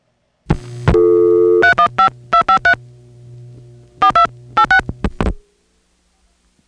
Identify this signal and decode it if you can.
DTMF, heard as telephone touch tones, tones of about 100 ms. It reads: 315253 42 86